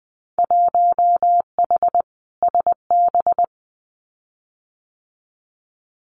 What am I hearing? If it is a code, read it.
Morse code, heard as a keyed tone, about 20 wpm, 700 Hz. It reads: 1H SB
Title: Winter Field Day Exchanges 20wpm